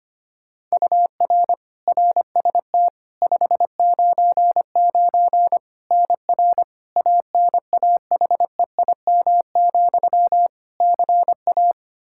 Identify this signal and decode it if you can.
Morse code, heard as a keyed tone, about 25 wpm, 700 Hz. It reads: UR RST 599 NR ANAHEIM, CA
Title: Practice QSO - Exchange 1 - Beginning - 25wpm